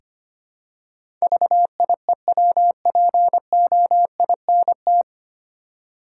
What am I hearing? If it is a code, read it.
Morse code, heard as a keyed tone, about 25 wpm, 700 Hz. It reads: VIEWPOINT
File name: Nine Letter Words - 25wpm